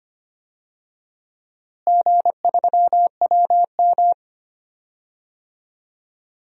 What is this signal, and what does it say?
Morse code, heard as a keyed tone, about 25 wpm, 700 Hz. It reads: G3WM
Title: DXCC-100 Call Signs - 4 Characters - Repeated 2x - 25wpm